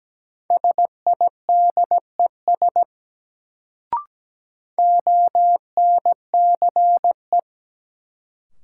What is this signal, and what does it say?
Morse code, heard as a keyed tone, about 17 wpm, 700 Hz. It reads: SIDES ONCE
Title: Sets of 4 Words - New Word Drill - 17wpm